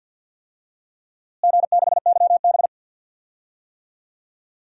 Morse code, heard as a keyed tone, about 50 wpm, 700 Hz. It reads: G6YB
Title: DXCC-100 Call Signs - 4 Characters - Repeated 3x - 50wpm